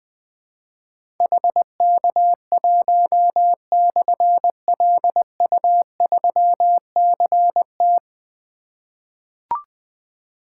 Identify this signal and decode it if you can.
Morse code, heard as a keyed tone, about 21 wpm, 700 Hz. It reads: HK1/LU3CT
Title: Difficult Call Signs 20wpm